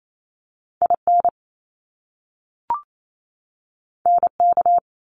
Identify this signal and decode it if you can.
Morse code, heard as a keyed tone, about 28 wpm, 700 Hz. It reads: IN NK
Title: Top Two Letter Combinations in Words 28wpm